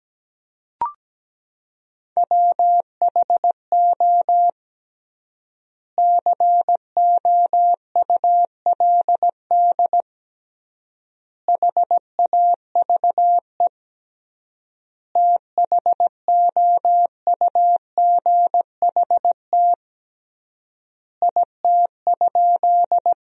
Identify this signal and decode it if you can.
Morse code, heard as a keyed tone, about 17 wpm, 700 Hz. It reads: WHO COULD HAVE THOUGHT IT?
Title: 3x Word-Spacing - Sentences from Top 300 Words - 17wpm